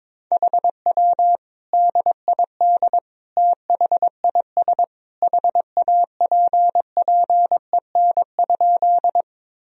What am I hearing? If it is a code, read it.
Morse code, heard as a keyed tone, about 21 wpm, 700 Hz. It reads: HW DID THIS HAPPEN?
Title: Sentences from Top 400 Words and Abbreviations 22wpm